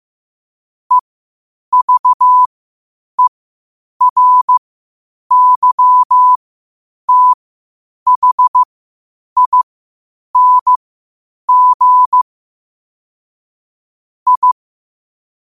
Morse code, heard as a keyed tone, about 15 wpm, spaced as for 8 wpm, 1000 Hz. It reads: EVERYTHING I
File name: qotdfiles / 8wpm